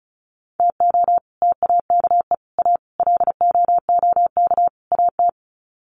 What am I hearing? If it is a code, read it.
Morse code, heard as a keyed tone, about 34 wpm, 700 Hz. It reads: TO TAKE A LOOK AT
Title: Sets of 5 Words 35wpm